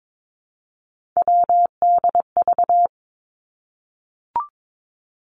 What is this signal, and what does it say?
Morse code, heard as a keyed tone, about 22 wpm, 700 Hz. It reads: WDV
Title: Three Challenging Letter-Number Combinations 22wpm